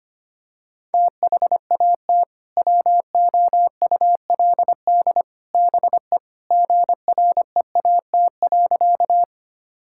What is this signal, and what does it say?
Morse code, heard as a keyed tone, about 25 wpm, 700 Hz. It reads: THAT WOULD BE GREAT.
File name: practice / 25wpm / Sentences with 2-Word Phrases - 25wpm